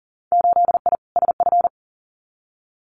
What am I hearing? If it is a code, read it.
Morse code, heard as a keyed tone, about 40 wpm, 700 Hz. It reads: 8I SF
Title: Winter Field Day Exchanges Encore 40wpm